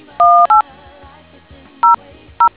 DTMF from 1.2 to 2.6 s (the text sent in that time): 00